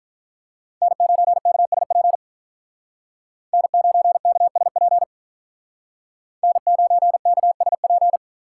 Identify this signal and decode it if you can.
Morse code, heard as a keyed tone, about 46 wpm, 700 Hz. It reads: N9KSP N9KSP N9KSP